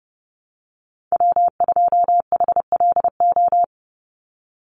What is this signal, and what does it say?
Morse code, heard as a keyed tone, about 30 wpm, 700 Hz. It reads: W2HLO
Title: USA Call Signs 30wpm